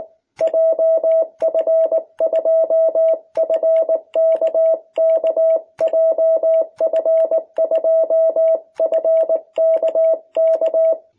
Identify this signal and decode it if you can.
Morse code, heard as a keyed tone, about 19 wpm, 660 Hz. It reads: JF2FKK JF2FKK
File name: jh8cbh / 2021cw